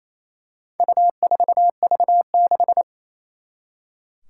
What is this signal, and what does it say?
Morse code, heard as a keyed tone, about 28 wpm, 700 Hz. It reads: U4V6
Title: Four Challenging Letter-Number Combinations 28wpm